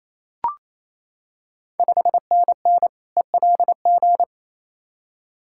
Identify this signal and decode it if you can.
Morse code, heard as a keyed tone, about 28 wpm, 700 Hz. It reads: 5NN ELG
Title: Ontario QSO Party Exchange 28wpm